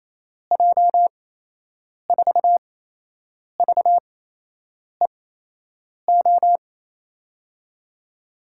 Morse code, heard as a keyed tone, about 28 wpm, spaced as for 8 wpm, 700 Hz. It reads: J4VEO
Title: USA Call Signs - 8x Character Spacing - 28wpm